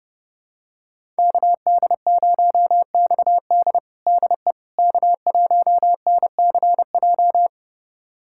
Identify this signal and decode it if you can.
Morse code, heard as a keyed tone, about 30 wpm, 700 Hz. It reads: KD0XD DE K1NCJ
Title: USA Callsign DE Callsign - 5 Characters - 30wpm